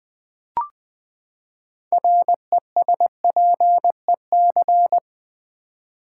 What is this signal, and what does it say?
Morse code, heard as a keyed tone, about 20 wpm, 700 Hz. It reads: RESPEC
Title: Build A Word - 6 Letters At A Time - 12-Letter Words - 20wpm